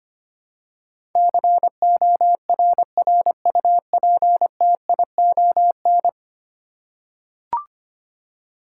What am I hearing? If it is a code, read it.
Morse code, heard as a keyed tone, about 25 wpm, 700 Hz. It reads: CORRUPTION
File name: 10 Letter Words - First 3 Letters Given - 25wpm